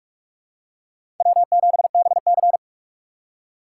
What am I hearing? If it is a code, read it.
Morse code, heard as a keyed tone, about 47 wpm, 700 Hz. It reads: W7BC
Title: USA Call Signs - 4 Characters - 6x Character Spacing - 45wpm